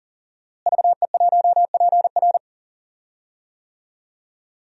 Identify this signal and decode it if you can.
Morse code, heard as a keyed tone, about 40 wpm, 700 Hz. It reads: VE1PR
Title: DXCC-100 Call Signs - 5 Characters - Repeated 3x - 40wpm